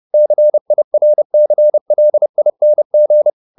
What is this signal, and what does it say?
Morse code, heard as a keyed tone, about 30 wpm, 600 Hz. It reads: CIRCLING